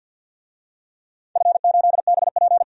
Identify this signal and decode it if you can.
Morse code, heard as a keyed tone, about 52 wpm, 700 Hz. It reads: F8BP
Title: DXCC-100 Call Signs - 4 Characters - 50wpm